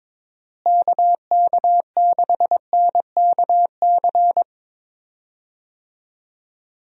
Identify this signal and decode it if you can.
Morse code, heard as a keyed tone, about 22 wpm, 700 Hz. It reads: KK6NKC
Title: USA Call Signs - 6 Characters - Repeated 3x - 22wpm